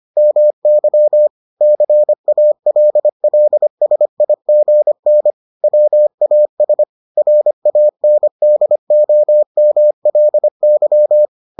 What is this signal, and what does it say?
Morse code, heard as a keyed tone, about 25 wpm, 600 Hz. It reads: MY CALLSIGN WAS RANDOMLY